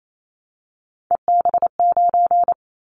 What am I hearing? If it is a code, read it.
Morse code, heard as a keyed tone, about 28 wpm, 700 Hz. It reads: EB9
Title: Callsign Prefixes for Top 100 DXCC - 28wpm